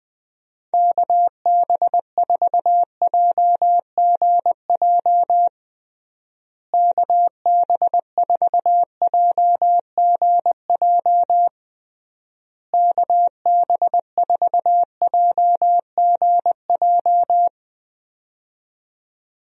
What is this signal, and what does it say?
Morse code, heard as a keyed tone, about 21 wpm, 700 Hz. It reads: KB4JGJ KB4JGJ KB4JGJ
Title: USA Call Signs - 6 Characters - Repeated 3x - 20wpm